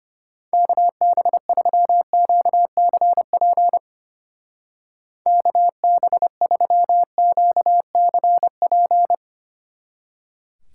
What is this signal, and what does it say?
Morse code, heard as a keyed tone, about 29 wpm, 700 Hz. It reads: KB3QCP KB3QCP
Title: Speed Racing - USA 2x3 Call Signs 25wpm